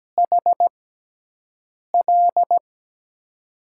Morse code, heard as a keyed tone, about 17 wpm, spaced as for 6 wpm, 700 Hz. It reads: HL
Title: USA Call Signs - 5 Characters - 6x Character Spacing - 17wpm